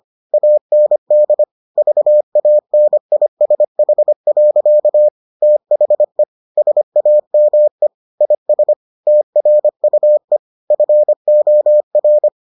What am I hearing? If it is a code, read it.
Morse code, heard as a keyed tone, about 25 wpm, 600 Hz. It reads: AND VANISH. THE SAME IS TRUE FOR